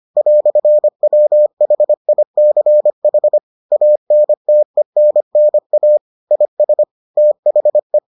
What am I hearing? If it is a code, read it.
Morse code, heard as a keyed tone, about 25 wpm, 600 Hz. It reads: "WHICH ANTENNA IS THE